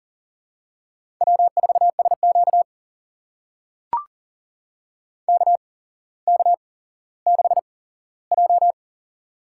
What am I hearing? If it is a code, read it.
Morse code, heard as a keyed tone, about 40 wpm, spaced as for 17 wpm, 700 Hz. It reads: W4SQ KK6J
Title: USA Call Signs - 4 Characters - 8x Character Spacing - 40wpm